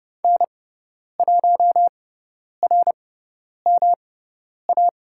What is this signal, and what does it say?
Morse code, heard as a keyed tone, about 30 wpm, spaced as for 10 wpm, 700 Hz. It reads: N1RMA
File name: Lesson 18g - Letter m - Call Signs 30wpm